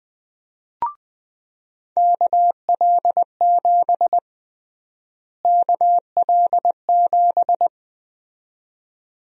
Morse code, heard as a keyed tone, about 20 wpm, 700 Hz. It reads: KL7 KL7
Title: USA Call Sign Prefixes - Repeated 2x - 20wpm